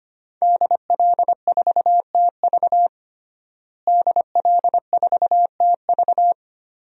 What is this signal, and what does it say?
Morse code, heard as a keyed tone, about 25 wpm, 700 Hz. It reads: DL4TV DL4TV
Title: DXCC-100 Call Signs - 5 Characters - Repeated 3x - 25wpm